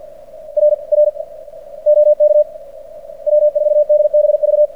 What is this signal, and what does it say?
Morse code, heard as a keyed tone, about 45 wpm, 600 Hz, with radio noise. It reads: R R OK W4DBV
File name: archived / CRSnet